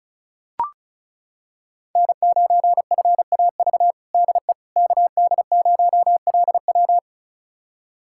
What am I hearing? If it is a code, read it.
Morse code, heard as a keyed tone, about 34 wpm, 700 Hz. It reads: N9FAV DE KD0LW
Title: USA Callsign DE Callsign - 5 Characters - 35wpm